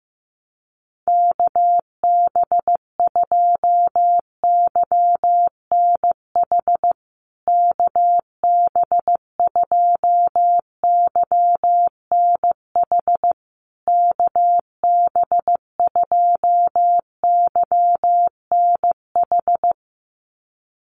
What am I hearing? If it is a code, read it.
Morse code, heard as a keyed tone, about 15 wpm, 700 Hz. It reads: KB2YNH KB2YNH KB2YNH
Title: USA Call Signs Repeated 3x 15wpm